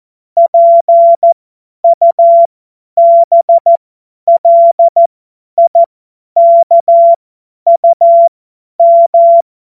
Morse code, heard as a keyed tone, about 14 wpm, spaced as for 10 wpm, 680 Hz. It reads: PUBLIKUM
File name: Morsekurs / Witze / Witz 04. Oktober 2022 50 F70